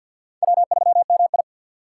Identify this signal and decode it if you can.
Morse code, heard as a keyed tone, about 46 wpm, 700 Hz. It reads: W3MI
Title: USA Call Signs - 4 Characters - 2x Character Spacing - 50wpm